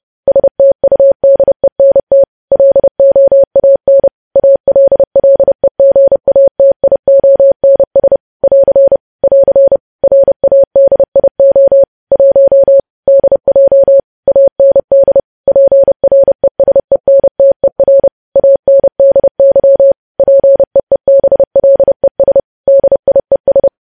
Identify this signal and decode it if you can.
Morse code, heard as a keyed tone, about 30 wpm, 550 Hz. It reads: STUDENT LOAN ALLEGATIONS <AR> <AR> RADIO 1 DJ AND PRESENTER ANDY PEEBLES DIES
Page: news headlines in morse code at 30 wpm